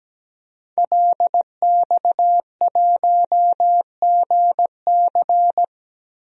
Morse code, heard as a keyed tone, about 17 wpm, 700 Hz. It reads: LX1GC
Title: DXCC-100 Call Signs - 5 Characters - 17wpm